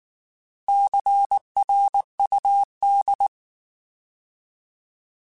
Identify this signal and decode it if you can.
Morse code, heard as a keyed tone, about 19 wpm, 800 Hz. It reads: CRUD